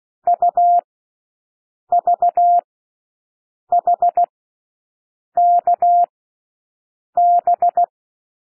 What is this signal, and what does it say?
Morse code, heard as a keyed tone, about 16 wpm, spaced as for 6 wpm, 700 Hz. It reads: UVHKB
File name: cwmp3 / L1 SOUND LETTERS 16 wpm